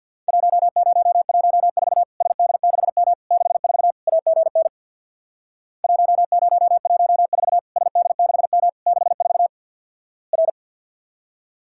Morse code, heard as a keyed tone, about 50 wpm, 700 Hz. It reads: 1014 SD6M 64 AGN 1014 SD6M 64 R
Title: Worked All Europe DX Contest - QTC Exchange with Repeats - 50wpm